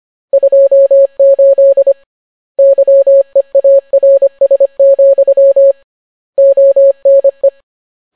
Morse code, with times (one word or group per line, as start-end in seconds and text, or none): 0.3-1.9 28
2.6-5.7 YEARS,
6.4-7.5 ONE